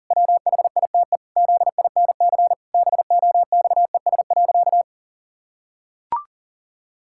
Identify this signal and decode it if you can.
Morse code, heard as a keyed tone, about 40 wpm, 700 Hz. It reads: WHITE ZINC BOXES.
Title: Pangram Challenge v2 - 40wpm